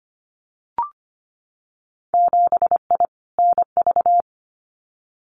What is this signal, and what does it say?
Morse code, heard as a keyed tone, about 25 wpm, 700 Hz. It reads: 7I NV
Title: Winter Field Day Exchanges Encore 25wpm